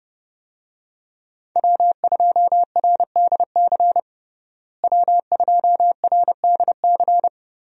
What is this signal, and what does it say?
Morse code, heard as a keyed tone, about 30 wpm, 700 Hz. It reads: W2RDC W2RDC
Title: USA Call Signs - 5 Characters - Repeated 3x - 30wpm